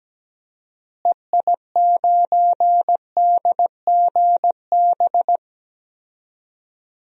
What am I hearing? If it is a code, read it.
Morse code, heard as a keyed tone, about 17 wpm, 700 Hz. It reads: EI9DGB
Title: DXCC-100 Call Signs - 6 Characters - Repeated 2x - 17wpm